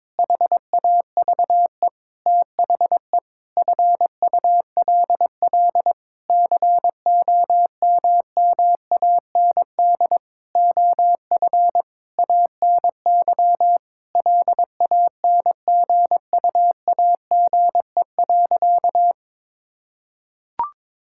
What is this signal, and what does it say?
Morse code, heard as a keyed tone, about 21 wpm, 700 Hz. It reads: HAVE THE FULL COMMAND OF ANY LANGUAGE.
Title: Sentences from Top 1600 Words 22wpm